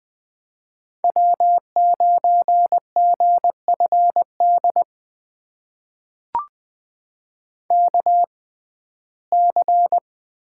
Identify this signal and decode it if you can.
Morse code, heard as a keyed tone, about 20 wpm, spaced as for 14 wpm, 700 Hz. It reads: W9GFD KC
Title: USA Call Signs - 5 Characters - 6x Character Spacing - 20wpm